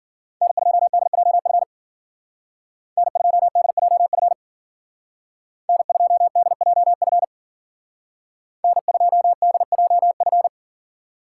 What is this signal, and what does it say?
Morse code, heard as a keyed tone, about 49 wpm, 700 Hz. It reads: N2DJF N2DJF N2DJF N2DJF
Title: Speed Racing - USA Call Signs 40wpm